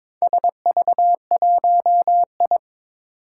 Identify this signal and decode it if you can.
Morse code, heard as a keyed tone, about 23 wpm, 700 Hz. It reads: SV1I
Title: Four Challenging Letter-Number Combinations 22wpm